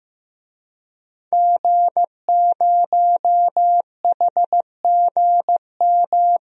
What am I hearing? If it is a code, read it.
Morse code, heard as a keyed tone, about 15 wpm, 700 Hz. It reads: G0HGM
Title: DXCC-100 Call Signs - 5 Characters - 15wpm